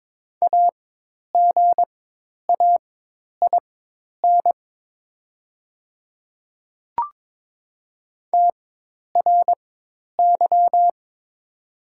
Morse code, heard as a keyed tone, about 22 wpm, spaced as for 10 wpm, 700 Hz. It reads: AGAIN TRY
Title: 4x Character Spacing - Top 300 Words - 22wpm